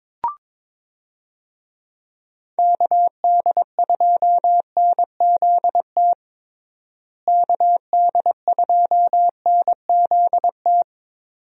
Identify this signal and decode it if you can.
Morse code, heard as a keyed tone, about 22 wpm, 700 Hz. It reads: KD2NZT KD2NZT
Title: USA Call Signs - 6 Characters - Repeated 2x - 22wpm